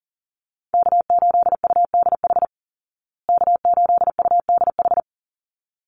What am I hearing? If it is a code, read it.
Morse code, heard as a keyed tone, about 40 wpm, 700 Hz. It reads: K8UDH K8UDH
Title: Call Signs Repeated 3x - 40wpm